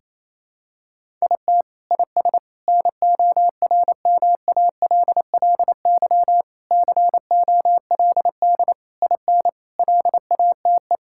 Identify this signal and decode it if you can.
Morse code, heard as a keyed tone, about 28 wpm, 700 Hz. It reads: IT IS NORMALLY COLD IN LATE
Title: Sentences from Top 1000 Words - Review Part 6 - 28wpm